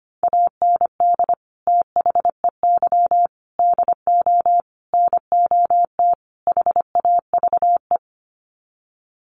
Morse code, heard as a keyed tone, about 25 wpm, 700 Hz. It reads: AND THEY DO NOT HAVE
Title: Sets of 5 Words Encore2 25wpm